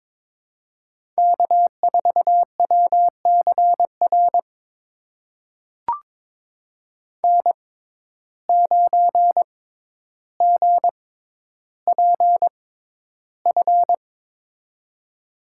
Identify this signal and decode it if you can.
Morse code, heard as a keyed tone, about 22 wpm, spaced as for 11 wpm, 700 Hz. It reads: K4WCR N9GPF
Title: USA Call Signs - 5 Characters - 6x Character Spacing - 22wpm